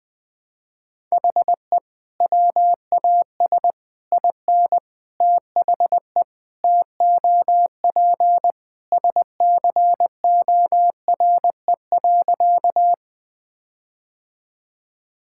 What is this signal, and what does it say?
Morse code, heard as a keyed tone, about 20 wpm, 700 Hz. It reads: HE WAS IN THE TOP SCORE.
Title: Sentences from Top 1600 Words 20wpm